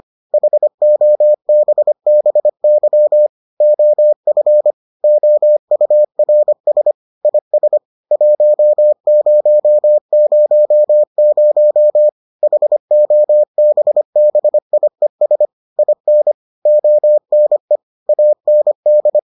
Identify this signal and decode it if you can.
Morse code, heard as a keyed tone, about 25 wpm, 600 Hz. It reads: HOBBY OF OURS IS 1000 HOBBIES IN ONE AND